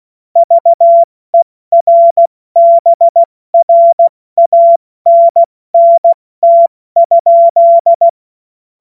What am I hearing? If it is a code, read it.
Morse code, heard as a keyed tone, about 16 wpm, 680 Hz. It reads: VERBRANNT?
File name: Witz-09-September-2025-70-F80